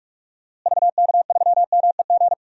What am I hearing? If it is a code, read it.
Morse code, heard as a keyed tone, about 46 wpm, 700 Hz. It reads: VK3MEG
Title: DXCC-100 Call Signs - 6 Characters - Repeated 3x - 45wpm